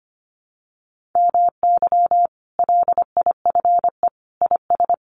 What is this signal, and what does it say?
Morse code, heard as a keyed tone, about 25 wpm, 700 Hz. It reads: MY LIFE IS
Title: Sentences from Top 1400 Words 25wpm